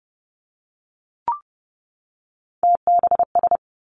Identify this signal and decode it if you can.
Morse code, heard as a keyed tone, about 30 wpm, 700 Hz. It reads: TBS
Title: Three Challenging Letter-Number Combinations 30wpm